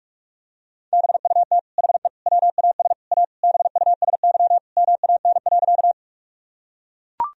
Morse code, heard as a keyed tone, about 41 wpm, 700 Hz. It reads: BUT HE WAS A BUSY MAN.
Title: Sentences from Top 1000 Words - Review Part 3 - 45wpm